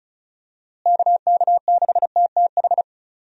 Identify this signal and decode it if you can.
Morse code, heard as a keyed tone, about 36 wpm, 700 Hz. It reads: KK6TTH
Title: USA Call Signs - 6 Characters - Repeated 2x - 35wpm